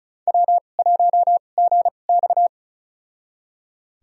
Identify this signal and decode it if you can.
Morse code, heard as a keyed tone, about 36 wpm, spaced as for 25 wpm, 700 Hz. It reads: W1GX
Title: USA Call Signs - 4 Characters - 2x Character Spacing - 35wpm